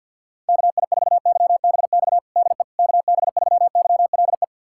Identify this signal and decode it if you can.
Morse code, heard as a keyed tone, about 45 wpm, 700 Hz. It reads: KI4YBX DE KB3YLE
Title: USA Callsign DE Callsign - 6 Characters - 50wpm